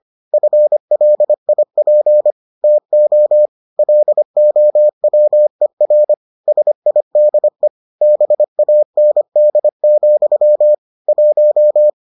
Morse code, heard as a keyed tone, about 25 wpm, 600 Hz. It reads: FLIP TO LOWER SIDE BAND, 1